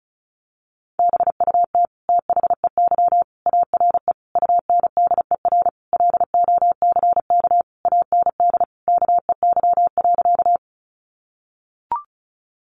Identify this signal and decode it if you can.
Morse code, heard as a keyed tone, about 34 wpm, 700 Hz. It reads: BUT THEY ARE UNDER LOCK AND KEY.
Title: Sentences from Top 1300 Words 35wpm